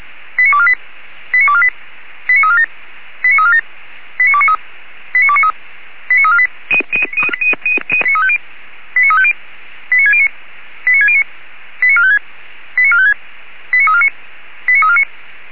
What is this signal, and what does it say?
Five-tone selective call (ZVEI), heard as quick runs of five tones, tones of about 70 ms. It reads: 89248 89248 89347 89347 89293 89293 89348 89350 89609 89609 89456 89456 89349 89349